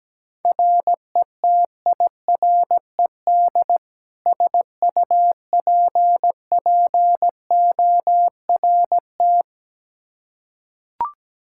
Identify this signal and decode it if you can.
Morse code, heard as a keyed tone, about 17 wpm, 700 Hz. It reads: RETIRED SUPPORT
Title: K1USN SST QSO Practice 17wpm